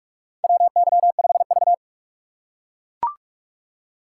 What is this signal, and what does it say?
Morse code, heard as a keyed tone, about 44 wpm, 700 Hz. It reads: WY5V
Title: USA Call Signs - 4 Characters - 3x Character Spacing - 45wpm